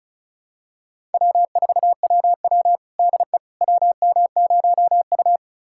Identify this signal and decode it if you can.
Morse code, heard as a keyed tone, about 36 wpm, 700 Hz. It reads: W4WW DE WM0U
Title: USA Callsign DE Callsign - 4 Characters - 35wpm